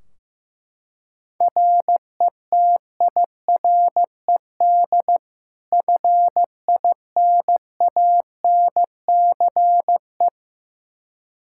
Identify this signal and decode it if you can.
Morse code, heard as a keyed tone, about 15 wpm, 700 Hz. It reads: RETIRED FINANCE
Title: K1USN SST QSO Practice 15wpm